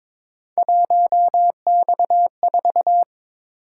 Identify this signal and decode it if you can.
Morse code, heard as a keyed tone, about 22 wpm, 700 Hz. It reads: 1X4
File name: practice / 22wpm / Three Letter-Number Combinations - 22wpm